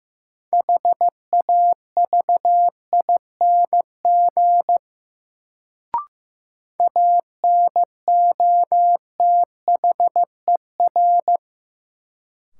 Morse code, heard as a keyed tone, about 15 wpm, 700 Hz. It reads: HAVING ANOTHER
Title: Sets of 5 Words Encore - New Word Drill - 15wpm